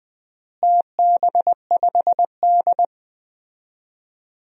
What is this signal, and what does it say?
Morse code, heard as a keyed tone, about 20 wpm, 700 Hz. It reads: TB5D
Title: DXCC-100 Call Signs - 4 Characters - Repeated 3x - 20wpm